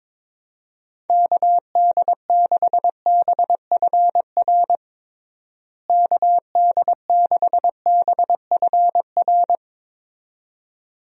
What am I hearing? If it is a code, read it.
Morse code, heard as a keyed tone, about 22 wpm, 700 Hz. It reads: KD6BFR KD6BFR
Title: USA Call Signs - 6 Characters - Repeated 2x - 22wpm